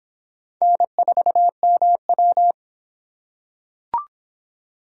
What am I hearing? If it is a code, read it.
Morse code, heard as a keyed tone, about 24 wpm, 700 Hz. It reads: N4MW